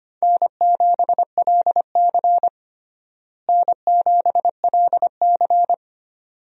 Morse code, heard as a keyed tone, about 25 wpm, 700 Hz. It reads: N7LC N7LC
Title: USA Call Signs - 4 Characters - Repeated 2x - 25wpm